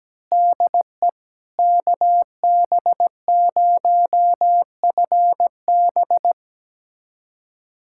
Morse code, heard as a keyed tone, about 17 wpm, 700 Hz. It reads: DE KB0FB
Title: USA Callsign DE Callsign - 5 Characters - 17wpm